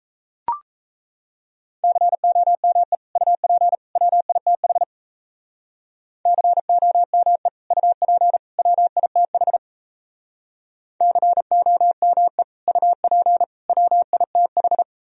Morse code, heard as a keyed tone, about 35 wpm, 700 Hz. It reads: COME UP WITH COME UP WITH COME UP WITH
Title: Speed-Racing - Sets of 3 Words 28wpm